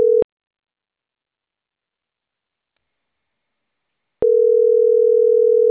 Ringback tone (North America), 440 + 480 Hz.